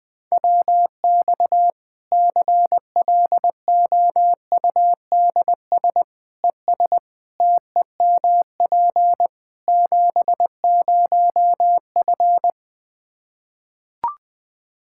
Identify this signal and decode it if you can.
Morse code, heard as a keyed tone, about 20 wpm, 700 Hz. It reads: WX CLOUDS ES TEMP 70F
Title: Practice QSO - Exchange 2 - Info - 20wpm